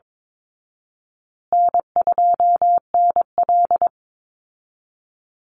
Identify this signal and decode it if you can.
Morse code, heard as a keyed tone, about 22 wpm, 700 Hz. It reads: N2NL
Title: Call Signs Repeated 3x - 22wpm